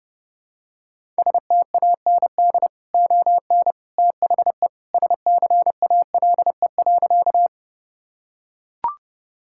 Morse code, heard as a keyed tone, about 30 wpm, 700 Hz. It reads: STAND ON THE SCALE.
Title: Sentences from Top 1000 Words - Review Part 5 - 30wpm